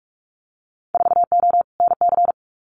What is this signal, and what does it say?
Morse code, heard as a keyed tone, about 44 wpm, 700 Hz. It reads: 4O NC
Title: Winter Field Day Exchanges Encore 45wpm